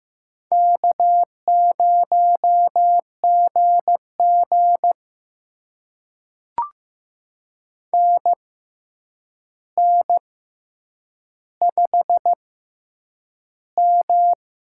Morse code, heard as a keyed tone, about 15 wpm, spaced as for 8 wpm, 700 Hz. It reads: K0GG NN5M
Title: USA Call Signs - 4 Characters - 6x Character Spacing - 15wpm